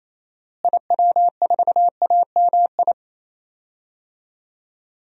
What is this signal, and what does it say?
Morse code, heard as a keyed tone, about 28 wpm, 700 Hz. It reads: IW4AMI
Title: DXCC-100 Call Signs - 6 Characters - Repeated 3x - 28wpm